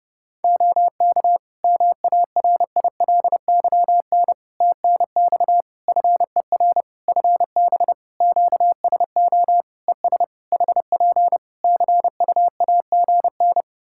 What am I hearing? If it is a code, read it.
Morse code, heard as a keyed tone, about 30 wpm, 700 Hz. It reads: OK MARILYN TNX FER FB QSO ES HP CUAGN
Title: Practice QSO - Exchange 4 - Ending - 30wpm